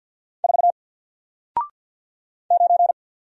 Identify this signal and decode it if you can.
Morse code, heard as a keyed tone, about 48 wpm, 700 Hz. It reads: 4 9
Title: Single Letter-Number Alt 50wpm